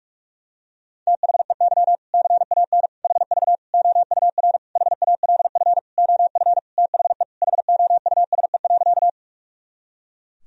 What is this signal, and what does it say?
Morse code, heard as a keyed tone, about 42 wpm, 700 Hz. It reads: THEY CAN HV OUR HALF OF THE HOUSE.
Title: Sentences from Top 400 Words and Abbreviations 45wpm